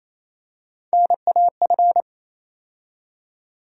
Morse code, heard as a keyed tone, about 28 wpm, 700 Hz. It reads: NAF
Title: USA Call Sign Postfixes - 3 Letters 28wpm